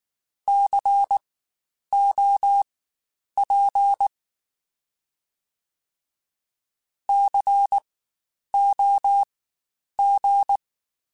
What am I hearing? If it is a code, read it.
Morse code, heard as a keyed tone, about 19 wpm, spaced as for 9 wpm, 800 Hz. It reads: COP COG
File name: Spaced Alphas / Spaced Alphas mp3s / Cs sp @18